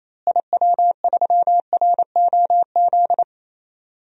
Morse code, heard as a keyed tone, about 28 wpm, 700 Hz. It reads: IW3ROZ
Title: DXCC-100 Call Signs - 6 Characters - Repeated 2x - 28wpm